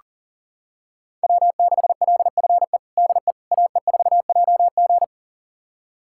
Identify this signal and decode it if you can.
Morse code, heard as a keyed tone, about 40 wpm, 700 Hz. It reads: W6LFE DE AE4JG
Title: USA Callsign DE Callsign - 5 Characters - 40wpm